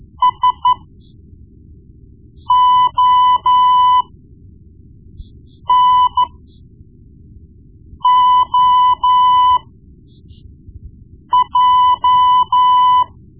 Morse code, heard as a keyed tone, about 13 wpm, spaced as for 4 wpm, 1000 Hz. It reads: SONOJ
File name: Morse Lessons / Morse Lessons / Tape-1B